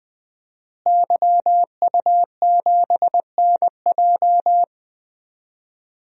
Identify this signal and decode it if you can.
Morse code, heard as a keyed tone, about 20 wpm, 700 Hz. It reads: YU7NJ
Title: DXCC-100 Call Signs - 5 Characters - Repeated 2x - 20wpm